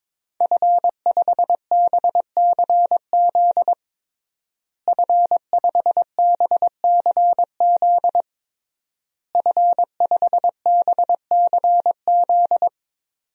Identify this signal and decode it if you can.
Morse code, heard as a keyed tone, about 22 wpm, 700 Hz. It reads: F5BCZ F5BCZ F5BCZ
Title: DXCC-100 Call Signs - 5 Characters - Repeated 3x - 22wpm